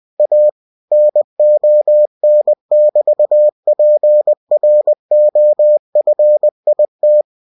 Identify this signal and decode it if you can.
Morse code, heard as a keyed tone, about 20 wpm, 600 Hz. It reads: A NON<BT>PROFIT